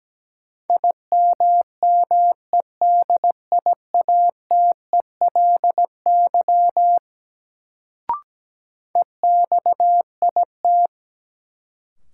Morse code, heard as a keyed tone, about 17 wpm, 700 Hz. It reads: IMMEDIATELY EXIT
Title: Sentences from Top 1100 Words - New Word Drill - 17wpm